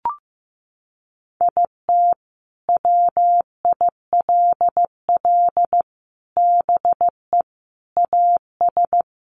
Morse code, heard as a keyed tone, about 15 wpm, 700 Hz. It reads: IT WILL BE AS
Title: Sentences from Top 1400 Words 15wpm